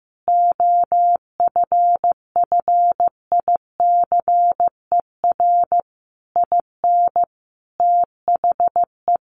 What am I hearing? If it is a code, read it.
Morse code, heard as a keyed tone, about 15 wpm, 700 Hz. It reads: OFFICER IN THE
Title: Sentences from Top 1200 Words 15wpm